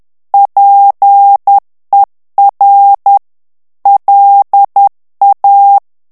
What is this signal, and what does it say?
Morse code, heard as a keyed tone, about 10 wpm, 800 Hz. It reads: PER LA
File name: morse mp3 / ITALIEN texte clair 1 vitesse 10 mots minute